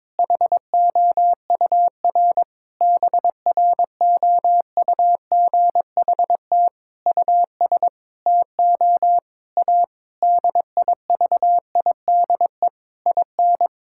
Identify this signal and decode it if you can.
Morse code, heard as a keyed tone, about 22 wpm, 700 Hz. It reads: HOUR BROUGHT US TO A DIVIDE IN
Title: Sentences from Top 1600 Words 22wpm